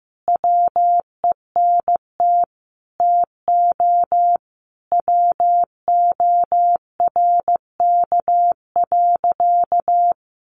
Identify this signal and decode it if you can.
Morse code, heard as a keyed tone, about 15 wpm, 700 Hz. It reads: WENT TO WORK.
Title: Sentences from Top 1300 Words 15wpm